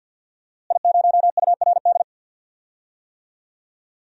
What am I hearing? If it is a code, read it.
Morse code, heard as a keyed tone, about 46 wpm, 700 Hz. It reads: I0URD
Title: DXCC-100 Call Signs - 5 Characters - Repeated 3x - 50wpm